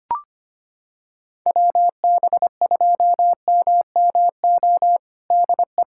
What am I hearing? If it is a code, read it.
Morse code, heard as a keyed tone, about 25 wpm, 700 Hz. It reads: WB2MMO DE
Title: USA Callsign DE Callsign - 6 Characters - 25wpm